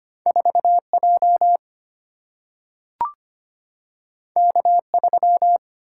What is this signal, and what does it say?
Morse code, heard as a keyed tone, about 25 wpm, 700 Hz. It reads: 4J K3
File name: Two Letter-Number Combinations - 25wpm